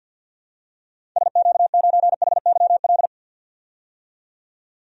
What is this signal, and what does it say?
Morse code, heard as a keyed tone, about 46 wpm, 700 Hz. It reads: SQ9HYL
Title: DXCC-100 Call Signs - 6 Characters - 50wpm